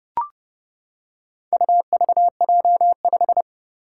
Morse code, heard as a keyed tone, about 30 wpm, 700 Hz. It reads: UVJ5
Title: Four Challenging Letter-Number Combinations 30wpm